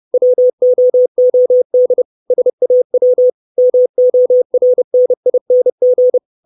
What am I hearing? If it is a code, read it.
Morse code, heard as a keyed tone, about 30 wpm, 500 Hz. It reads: WOOD SAW MORNING
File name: 632 Words 30WPM